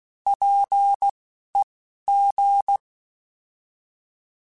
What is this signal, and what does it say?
Morse code, heard as a keyed tone, about 16 wpm, spaced as for 11 wpm, 800 Hz. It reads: PEG
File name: Alphabetical / Alphabetical mp3s / Ps @10